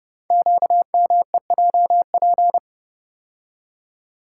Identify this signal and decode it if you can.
Morse code, heard as a keyed tone, about 30 wpm, 700 Hz. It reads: QMEJP
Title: Code Groups - 5 Letter:Numbers - 30wpm